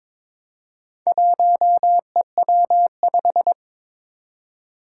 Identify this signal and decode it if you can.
Morse code, heard as a keyed tone, about 23 wpm, 700 Hz. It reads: 1EW5
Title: Four Challenging Letter-Number Combinations 22wpm